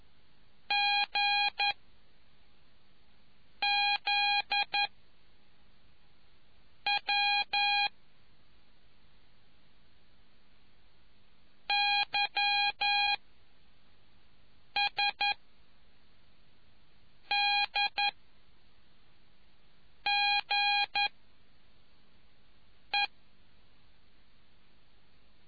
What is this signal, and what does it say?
Morse code, heard as a keyed tone, about 11 wpm, 820 Hz. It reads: G Z W Y S D G E